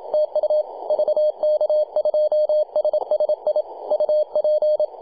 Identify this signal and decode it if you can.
Morse code, heard as a keyed tone, about 28 wpm, 610 Hz. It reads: TU VK2SSI UP
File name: posts / 2015-july-26-0904z-vk2ssi-18-074mhz-cw